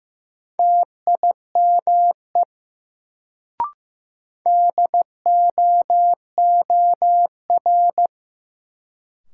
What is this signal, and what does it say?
Morse code, heard as a keyed tone, about 15 wpm, 700 Hz. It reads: TIME DOOR
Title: Sets of 2 Words - New Word Drill - 15wpm